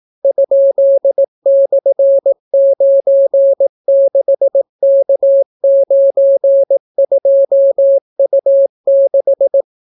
Morse code, heard as a keyed tone, about 17 wpm, 550 Hz. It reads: ?/96K92U6